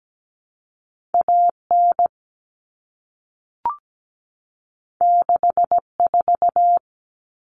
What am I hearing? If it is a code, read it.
Morse code, heard as a keyed tone, about 17 wpm, 700 Hz. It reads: AN 64
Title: Common Cut Numbers - 2-Digit 17wpm